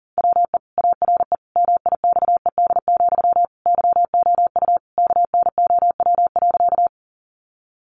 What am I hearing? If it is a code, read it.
Morse code, heard as a keyed tone, about 37 wpm, 700 Hz. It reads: WE ARE MIXED, YOU KNOW.
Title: Sentences from Top 1500 Words 40wpm